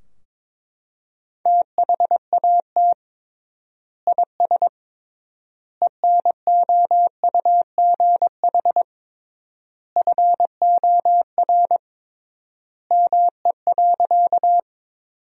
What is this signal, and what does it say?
Morse code, heard as a keyed tone, about 22 wpm, 700 Hz. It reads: THAT IS ENOUGH FOR ME.
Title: 3x Word-Spacing - Sentences from Top 400 Words - 22wpm